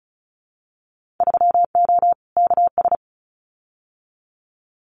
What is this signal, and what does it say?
Morse code, heard as a keyed tone, about 34 wpm, 700 Hz. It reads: 3O KS
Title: Winter Field Day Exchanges 35wpm